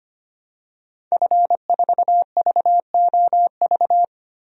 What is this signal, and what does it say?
Morse code, heard as a keyed tone, about 25 wpm, 700 Hz. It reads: F4VOV
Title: DXCC-100 Call Signs - 5 Characters - 25wpm